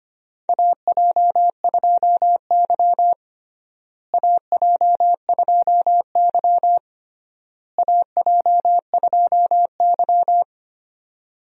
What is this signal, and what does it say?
Morse code, heard as a keyed tone, about 25 wpm, 700 Hz. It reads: AJ2Y AJ2Y AJ2Y
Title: USA Call Signs - 4 Characters - Repeated 3x - 25wpm